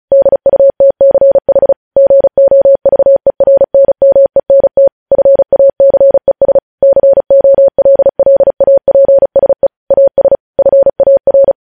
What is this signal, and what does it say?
Morse code, heard as a keyed tone, about 36 wpm, 550 Hz. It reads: DUTCH GOVERNMENT FACES COLLAPSE AS FAR
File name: podcast / News headlines-1x-35x35